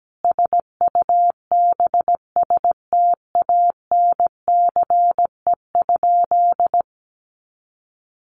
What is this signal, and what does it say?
Morse code, heard as a keyed tone, about 17 wpm, 700 Hz. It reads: SUBSTANCE?
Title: Sentences from Top 1500 Words 17wpm